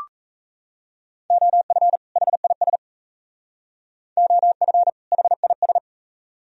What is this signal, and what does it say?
Morse code, heard as a keyed tone, about 35 wpm, 700 Hz. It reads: OF HIS OF HIS
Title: Speed-Racing - Sets of 2 Words 28wpm